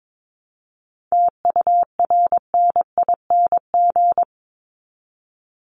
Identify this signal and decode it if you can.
Morse code, heard as a keyed tone, about 22 wpm, 700 Hz. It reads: TURNING
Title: ING Suffix - Top 100 Words with Base Words 22wpm